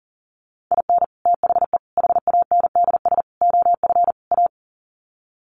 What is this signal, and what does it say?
Morse code, heard as a keyed tone, about 40 wpm, 700 Hz. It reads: IN THE HANDS OF A
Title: Sets of 5 Words Encore 40wpm